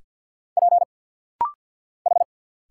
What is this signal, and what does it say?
Morse code, heard as a keyed tone, about 50 wpm, 690 Hz. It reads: P H
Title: Single Letters - Rapid-Fire 50wpm